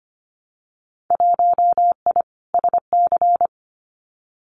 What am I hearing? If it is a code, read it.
Morse code, heard as a keyed tone, about 25 wpm, 700 Hz. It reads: 1I SC
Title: Winter Field Day Exchanges Encore 25wpm